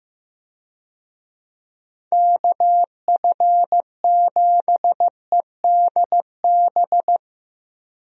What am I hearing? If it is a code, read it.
Morse code, heard as a keyed tone, about 15 wpm, 700 Hz. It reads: KF7EDB
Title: USA Call Signs - 6 Characters - Repeated 2x - 15wpm